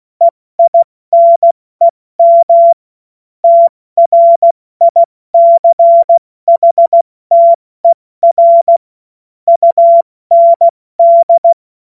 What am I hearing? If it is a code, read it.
Morse code, heard as a keyed tone, about 16 wpm, 680 Hz. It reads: EINEM TRICHTER UND
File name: Morsekurs / Witze / Witz 16. August 70 F80